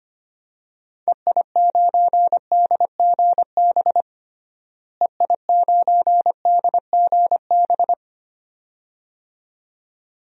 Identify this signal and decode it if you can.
Morse code, heard as a keyed tone, about 25 wpm, 700 Hz. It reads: EI9DGB EI9DGB
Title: DXCC-100 Call Signs - 6 Characters - Repeated 2x - 25wpm